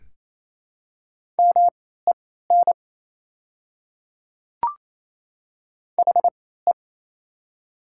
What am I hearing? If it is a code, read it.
Morse code, heard as a keyed tone, about 28 wpm, spaced as for 16 wpm, 700 Hz. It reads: MEN HE
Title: 3x Character Spacing - Top 200 Words - 28wpm